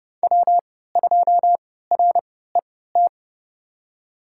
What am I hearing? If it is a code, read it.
Morse code, heard as a keyed tone, about 30 wpm, spaced as for 17 wpm, 700 Hz. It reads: W2RET
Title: USA Call Signs - 5 Characters - 3x Character Spacing - 30wpm